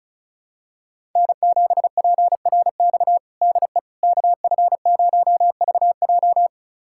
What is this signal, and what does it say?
Morse code, heard as a keyed tone, about 36 wpm, 700 Hz. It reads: N7PRX DE KF0VJ
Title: USA Callsign DE Callsign - 5 Characters - 35wpm